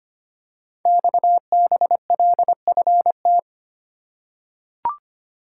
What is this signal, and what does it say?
Morse code, heard as a keyed tone, about 25 wpm, 700 Hz. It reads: XBLFT
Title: Code Groups - 5 Letters - 25wpm